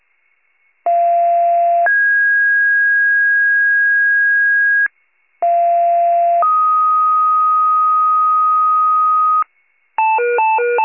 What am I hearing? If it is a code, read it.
Two-tone page, heard as a one-second tone, then a three-second tone, about 690 then 1650 Hz, about 690 then 1250 Hz, with noise. It is heard from 0.9 to 9.4 s.
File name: wcalert